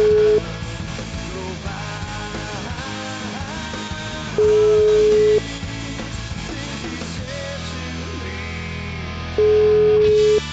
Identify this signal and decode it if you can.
Ringback tone (Europe, 425 Hz), with noise.